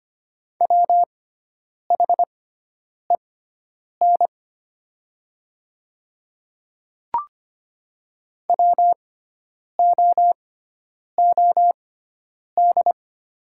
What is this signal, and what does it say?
Morse code, heard as a keyed tone, about 25 wpm, spaced as for 9 wpm, 700 Hz. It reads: WHEN WOOD
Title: 6x Character Spacing - Top 500 Words - 25wpm